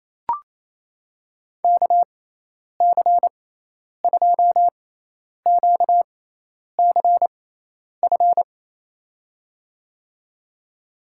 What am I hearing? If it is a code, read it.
Morse code, heard as a keyed tone, about 28 wpm, spaced as for 10 wpm, 700 Hz. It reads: KC2QCF
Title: USA Call Signs - 6x Character Spacing - 28wpm